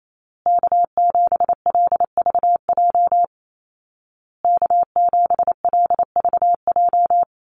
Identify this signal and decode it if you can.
Morse code, heard as a keyed tone, about 28 wpm, 700 Hz. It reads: K7LVJ K7LVJ
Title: Call Signs Repeated 2x - 28wpm